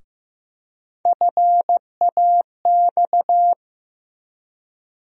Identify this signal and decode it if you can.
Morse code, heard as a keyed tone, about 15 wpm, 700 Hz. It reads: FAX
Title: USA Call Sign Postfixes - 3 Letters 15wpm